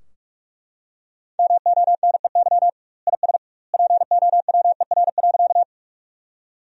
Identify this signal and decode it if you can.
Morse code, heard as a keyed tone, about 41 wpm, 700 Hz. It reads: MONEY IS POWER.
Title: 2x Word-Spacing - Sentences from Top 500 Words - 45wpm